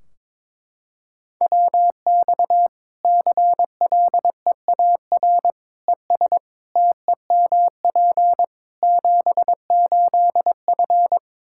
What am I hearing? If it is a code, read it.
Morse code, heard as a keyed tone, about 22 wpm, 700 Hz. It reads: WX CLEAR ES TEMP 78F
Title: K1USN SST QSO Practice 22wpm